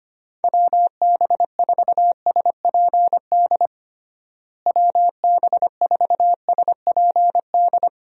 Morse code, heard as a keyed tone, about 25 wpm, 700 Hz. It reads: WB4SPD WB4SPD
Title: USA Call Signs - 6 Characters - Repeated 2x - 25wpm